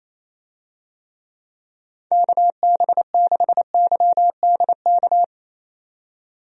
Morse code, heard as a keyed tone, about 27 wpm, 700 Hz. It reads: KB6YDK
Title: USA Call Signs - 6 Characters - 28wpm